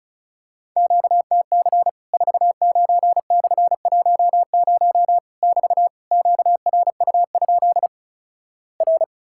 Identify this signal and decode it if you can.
Morse code, heard as a keyed tone, about 34 wpm, 700 Hz. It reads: QTC 49/10 <BT> QRU? R
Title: Worked All Europe DX Contest - QTC Exchange - 35wpm